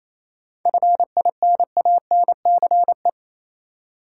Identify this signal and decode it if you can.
Morse code, heard as a keyed tone, about 28 wpm, 700 Hz. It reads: FINANCE
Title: Seven Letter Words - 28wpm